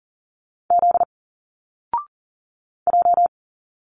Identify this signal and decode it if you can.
Morse code, heard as a keyed tone, about 40 wpm, 700 Hz. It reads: Z J
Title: Single Letters 40wpm